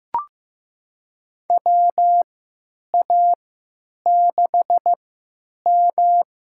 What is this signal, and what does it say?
Morse code, heard as a keyed tone, about 15 wpm, spaced as for 8 wpm, 700 Hz. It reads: WA6M
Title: USA Call Signs - 3x Character Spacing - 15wpm